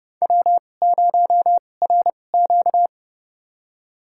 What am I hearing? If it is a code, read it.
Morse code, heard as a keyed tone, about 30 wpm, spaced as for 22 wpm, 700 Hz. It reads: W0RQ
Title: USA Call Signs - 4 Characters - 2x Character Spacing - 30wpm